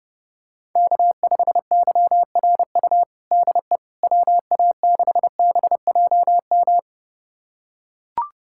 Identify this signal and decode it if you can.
Morse code, heard as a keyed tone, about 30 wpm, 700 Hz. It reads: K5YRU DE WA6BJM
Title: USA Callsign DE Callsign 30wpm